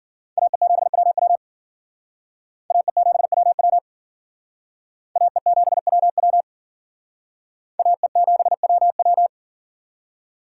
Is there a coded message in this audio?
Yes: Morse code from 0.4 to 9.3 s, heard as a keyed tone, about 52 wpm, 700 Hz. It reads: AE7WW AE7WW AE7WW AE7WW